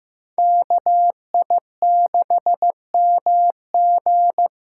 Morse code, heard as a keyed tone, about 15 wpm, 700 Hz. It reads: KI6MG
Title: USA Callsign DE Callsign 15wpm